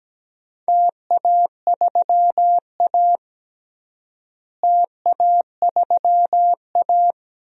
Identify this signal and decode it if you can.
Morse code, heard as a keyed tone, about 17 wpm, 700 Hz. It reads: TA3A TA3A
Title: DXCC-100 Call Signs - 4 Characters - Repeated 2x - 17wpm